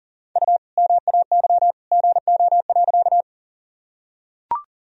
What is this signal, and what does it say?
Morse code, heard as a keyed tone, about 40 wpm, 700 Hz. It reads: U MAY GO.
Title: Sentences from Top 100 Words and Abbreviations 40wpm